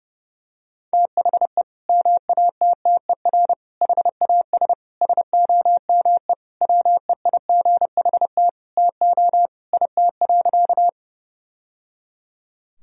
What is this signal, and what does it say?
Morse code, heard as a keyed tone, about 30 wpm, 700 Hz. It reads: THE MATTER HAS SOME WEIGHT TO IT.
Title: Sentences from Top 1000 Words - Review Part 3 - 30wpm